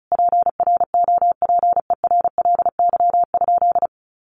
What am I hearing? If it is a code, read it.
Morse code, heard as a keyed tone, about 34 wpm, 700 Hz. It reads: PROPERLY?
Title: Sentences from Top 1100 Words 35wpm